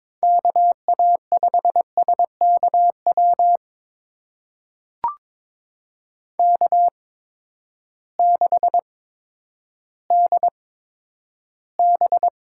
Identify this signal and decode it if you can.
Morse code, heard as a keyed tone, about 22 wpm, spaced as for 11 wpm, 700 Hz. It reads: KA5SKW K6DB
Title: USA Call Signs - 8x Character Spacing - 22wpm